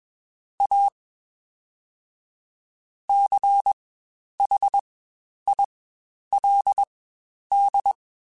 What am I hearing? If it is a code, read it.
Morse code, heard as a keyed tone, about 21 wpm, spaced as for 10 wpm, 800 Hz. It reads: A CHILD
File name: Quotes Spaced / Quotes Spaced-mp3s / ChildAsleep sp @20